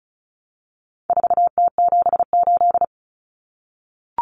Morse code, heard as a keyed tone, about 34 wpm, 700 Hz. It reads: 4T78
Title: Common Cut Numbers - 4-Digit 35wpm